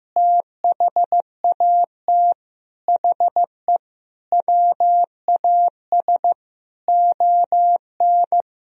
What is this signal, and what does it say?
Morse code, heard as a keyed tone, about 15 wpm, 700 Hz. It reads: THAT HE WAS ON
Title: Sentences from Top 500 Words and Abbreviations 15wpm